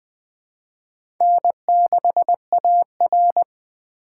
Morse code, heard as a keyed tone, about 21 wpm, 700 Hz. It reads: N6AR
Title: Speed Racing - USA 1x2 Call Signs 17wpm